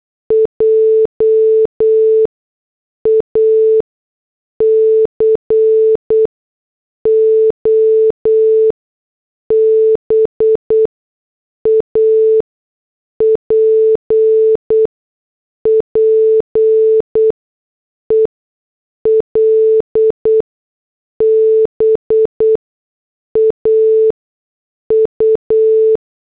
Morse code, heard as a keyed tone, about 8 wpm, 440 Hz. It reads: J A C O B A P P E L B A U
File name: morse / headlines